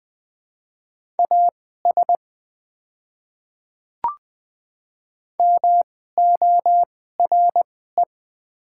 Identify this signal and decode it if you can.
Morse code, heard as a keyed tone, about 20 wpm, spaced as for 14 wpm, 700 Hz. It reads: AS MORE
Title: 2x Character Spacing - Top 100 Words - 20wpm